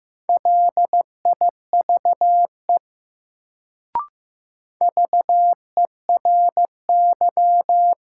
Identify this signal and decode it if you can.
Morse code, heard as a keyed tone, about 16 wpm, 700 Hz. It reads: LIVE VERY
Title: Sentences from Top 200 Words - New Word Drill - 15wpm